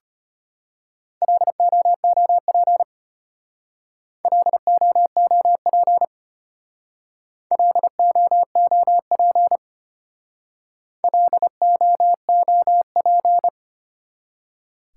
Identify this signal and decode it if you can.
Morse code, heard as a keyed tone, about 33 wpm, 700 Hz. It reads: LOOP LOOP LOOP LOOP